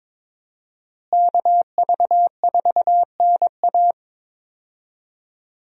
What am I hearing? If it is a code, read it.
Morse code, heard as a keyed tone, about 22 wpm, 700 Hz. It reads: KV4NA
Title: USA Call Signs - 5 Characters - Repeated 2x - 22wpm